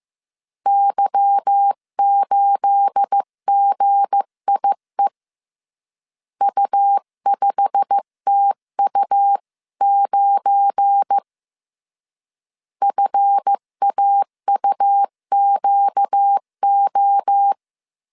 Morse code, heard as a keyed tone, about 15 wpm, 800 Hz. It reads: Y8GIE U5TU9 FAUQO
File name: CW / DET / 1 9 1500C S B 14wpm 3302